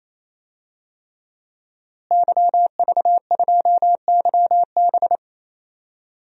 Morse code, heard as a keyed tone, about 28 wpm, 700 Hz. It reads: YV2YB
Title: DXCC-100 Call Signs - 5 Characters - Repeated 3x - 28wpm